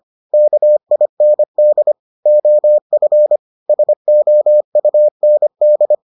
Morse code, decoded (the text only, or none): KIND OF SOUND